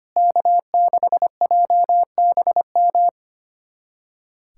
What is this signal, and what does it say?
Morse code, heard as a keyed tone, about 25 wpm, 700 Hz. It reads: K6JBM
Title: USA Callsign DE Callsign - 5 Characters - 25wpm